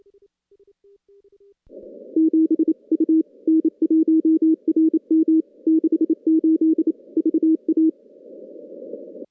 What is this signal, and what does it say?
Morse code, heard as a keyed tone, about 28 wpm, 320 Hz. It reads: 7 U N1RM 68 VA
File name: contests / CW-SS-k5tr- -2100